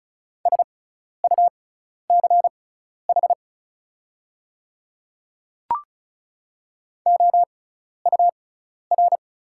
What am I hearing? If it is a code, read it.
Morse code, heard as a keyed tone, about 36 wpm, spaced as for 12 wpm, 690 Hz. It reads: SUCH OUR
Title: 6x Character Spacing - Top 300 Words - 35wpm